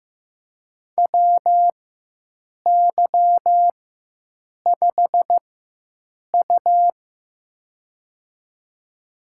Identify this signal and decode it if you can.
Morse code, heard as a keyed tone, about 15 wpm, spaced as for 7 wpm, 700 Hz. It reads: WY5U
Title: USA Call Signs - 4 Characters - 4x Character Spacing - 15wpm